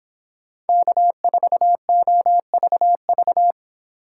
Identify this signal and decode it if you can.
Morse code, heard as a keyed tone, about 25 wpm, 700 Hz. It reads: K4OVV